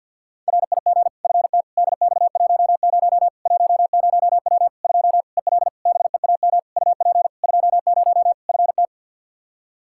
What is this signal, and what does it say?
Morse code, heard as a keyed tone, about 48 wpm, 700 Hz. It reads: RIG FT DX10 10W 2 EL BEAM UP 20 FT
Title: Practice QSO - Exchange 2 - Info - 50wpm